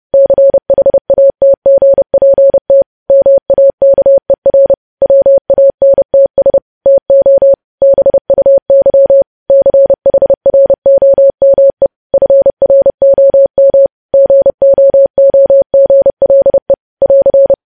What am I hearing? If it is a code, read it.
Morse code, heard as a keyed tone, about 30 wpm, 550 Hz. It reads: CHATGPT MAKER WANTS TO BUY CHROME FROM GOOGLE <AR>
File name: podcast / News headlines-1x-30x30